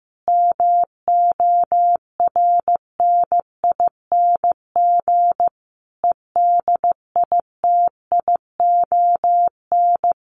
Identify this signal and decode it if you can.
Morse code, heard as a keyed tone, about 15 wpm, 700 Hz. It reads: MORNING EDITION
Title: Sets of 5 Words Encore2 15wpm